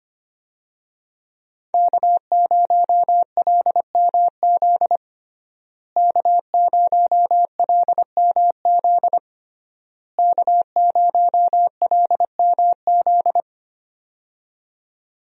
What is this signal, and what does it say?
Morse code, heard as a keyed tone, about 25 wpm, 700 Hz. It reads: K0LMZ K0LMZ K0LMZ
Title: USA Call Signs - 5 Characters - Repeated 3x - 25wpm